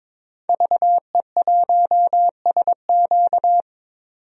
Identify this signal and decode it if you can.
Morse code, heard as a keyed tone, about 22 wpm, 700 Hz. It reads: VE1SQ
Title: DXCC-100 Call Signs - 5 Characters - 22wpm